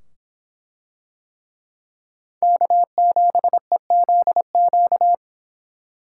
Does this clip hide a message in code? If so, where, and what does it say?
Yes: Morse code from 2.4 to 5.1 s, heard as a keyed tone, about 24 wpm, 700 Hz. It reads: K7EZQ